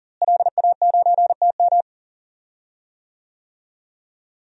Morse code, heard as a keyed tone, about 40 wpm, 700 Hz. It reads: LA9TM
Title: DXCC-100 Call Signs - 5 Characters - Repeated 2x - 40wpm